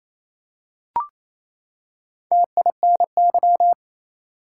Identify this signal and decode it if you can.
Morse code, heard as a keyed tone, about 28 wpm, 700 Hz. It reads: TINY
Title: Four Letter Words - 28wpm